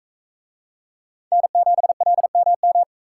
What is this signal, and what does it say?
Morse code, heard as a keyed tone, about 40 wpm, 700 Hz. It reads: N7LMM